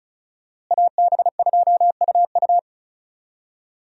Morse code, heard as a keyed tone, about 36 wpm, 700 Hz. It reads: AB2UU
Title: USA Call Signs - 5 Characters - Repeated 3x - 35wpm